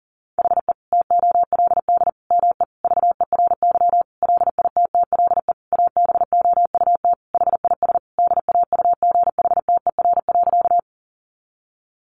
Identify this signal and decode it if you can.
Morse code, heard as a keyed tone, about 40 wpm, 700 Hz. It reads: HE TOLD ME VERY LITTLE ABOUT HIS DAUGHTER.
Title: Sentences from Top 1500 Words 40wpm